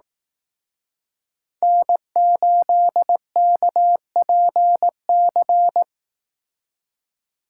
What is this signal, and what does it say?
Morse code, heard as a keyed tone, about 18 wpm, 700 Hz. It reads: N8KPC
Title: Speed Racing - USA Call Signs 15wpm